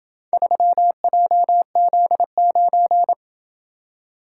Morse code, heard as a keyed tone, about 27 wpm, 700 Hz. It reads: 3JZ9